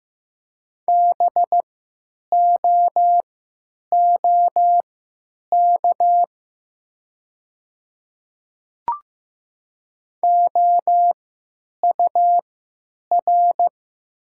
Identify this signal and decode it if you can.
Morse code, heard as a keyed tone, about 15 wpm, spaced as for 9 wpm, 700 Hz. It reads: BOOK OUR
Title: 3x Character Spacing - Top 300 Words - 15wpm